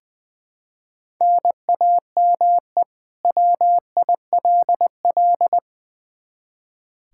Morse code, heard as a keyed tone, about 20 wpm, 700 Hz. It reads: NAME WILL
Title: K1USN SST QSO Practice 20wpm